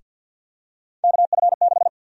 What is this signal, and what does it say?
Morse code, heard as a keyed tone, about 52 wpm, 700 Hz. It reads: KF6
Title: USA Call Sign Prefixes - 50wpm